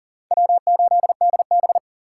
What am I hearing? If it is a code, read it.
Morse code, heard as a keyed tone, about 37 wpm, 700 Hz. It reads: W8DB
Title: USA Call Signs - 4 Characters - 6x Character Spacing - 40wpm